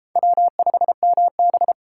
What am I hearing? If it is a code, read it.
Morse code, heard as a keyed tone, about 30 wpm, 700 Hz. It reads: W5MB